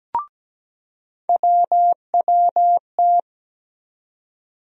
Morse code, heard as a keyed tone, about 17 wpm, 700 Hz. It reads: WWT
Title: USA Call Sign Postfixes - 3 Letters 17wpm